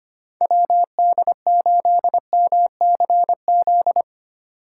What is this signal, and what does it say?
Morse code, heard as a keyed tone, about 25 wpm, 700 Hz. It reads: WD8MCZ
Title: USA Call Signs - 6 Characters - 3x Character Spacing - 25wpm